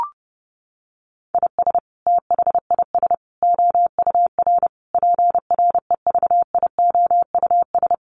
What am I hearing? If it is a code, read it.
Morse code, heard as a keyed tone, about 30 wpm, 700 Hz. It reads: IS THIS OUR PREVIOUS
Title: Sentences from Top 1500 Words 30wpm